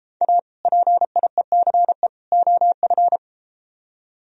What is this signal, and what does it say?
Morse code, heard as a keyed tone, about 32 wpm, 700 Hz. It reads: A PIECE OF